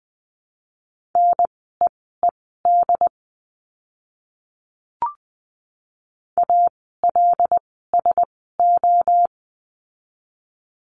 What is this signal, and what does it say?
Morse code, heard as a keyed tone, about 20 wpm, spaced as for 15 wpm, 700 Hz. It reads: NEED ALSO
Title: 101 Lesson 17f - Letter u - Words 20wpm